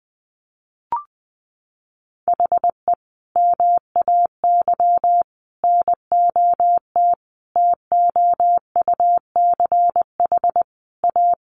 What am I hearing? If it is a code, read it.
Morse code, heard as a keyed tone, about 20 wpm, 700 Hz. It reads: HE MAY NOT TOUCH A
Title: Sentences from Top 1500 Words 20wpm